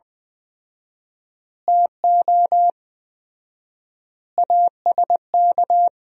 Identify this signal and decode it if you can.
Morse code, heard as a keyed tone, about 20 wpm, 700 Hz. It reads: TO ASK
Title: 4x Word-Spacing - Sentences from Top 200 Words - 20wpm